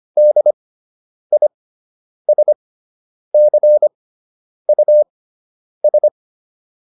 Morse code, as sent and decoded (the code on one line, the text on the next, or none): -.. .. ... -.-. ..- ...
DISCUS